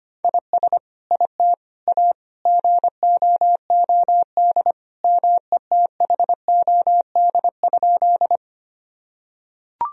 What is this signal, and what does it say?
Morse code, heard as a keyed tone, about 25 wpm, 700 Hz. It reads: IS IT A GOOD METHOD?
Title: Sentences from Top 1000 Words - Review Part 3 - 25wpm